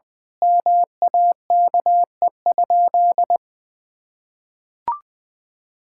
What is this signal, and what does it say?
Morse code, heard as a keyed tone, about 21 wpm, 700 Hz. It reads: MAKE?
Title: Sentences with 8-Word Phrases - 20wpm